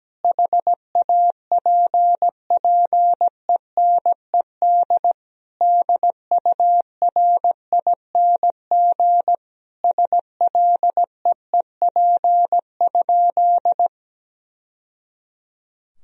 Morse code, heard as a keyed tone, about 17 wpm, 700 Hz. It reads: HAPPENED DURING SLEEP?
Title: Sentences from Top 1000 Words - Review Part 2 - 17wpm